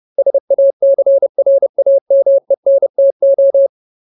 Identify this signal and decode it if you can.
Morse code, heard as a keyed tone, about 30 wpm, 550 Hz. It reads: SACRAMENTO
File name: Ultimate Word List / 30WPM Ultimate